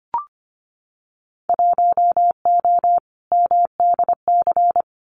Morse code, heard as a keyed tone, about 25 wpm, 700 Hz. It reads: 1O MDC
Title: Winter Field Day Exchanges 25wpm